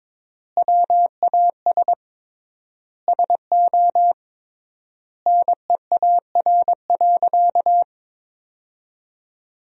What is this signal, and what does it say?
Morse code, heard as a keyed tone, about 22 wpm, 700 Hz. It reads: WAS SO NEAR.
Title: 3x Word-Spacing - Sentences from Top 300 Words - 22wpm